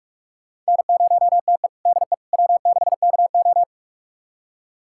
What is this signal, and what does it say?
Morse code, heard as a keyed tone, about 44 wpm, 700 Hz. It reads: N0TE DE W6KO
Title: USA Callsign DE Callsign - 4 Characters - 45wpm